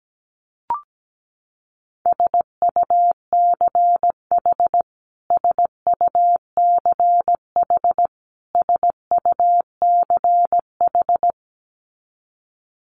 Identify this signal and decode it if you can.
Morse code, heard as a keyed tone, about 17 wpm, 700 Hz. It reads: SUCH SUCH SUCH
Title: Top 200 Words Repeated 17wpm